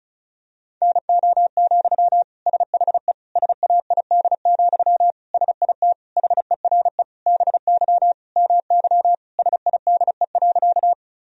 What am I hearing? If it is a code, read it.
Morse code, heard as a keyed tone, about 34 wpm, 700 Hz. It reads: NO, SHE SAID, SIT HERE BY MY SIDE.
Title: Sentences from Top 1000 Words - Review Part 1 - 35wpm